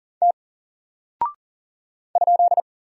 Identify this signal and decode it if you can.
Morse code, heard as a keyed tone, about 38 wpm, 700 Hz. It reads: T ?
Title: Single Character (Letter:Number:Punctuation:Prosign) Alt 40wpm